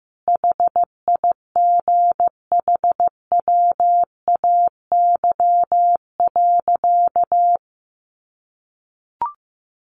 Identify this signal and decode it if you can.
Morse code, heard as a keyed tone, about 15 wpm, 700 Hz. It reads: HIGHWAY.
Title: Sentences from Top 1400 Words 15wpm